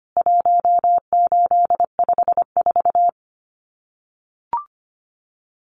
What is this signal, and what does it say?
Morse code, heard as a keyed tone, about 25 wpm, 700 Hz. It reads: 1854
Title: CWT Exchange 25wpm